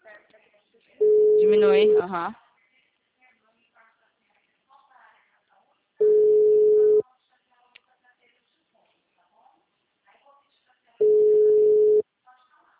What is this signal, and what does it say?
Ringback tone (Europe, 425 Hz), with speech.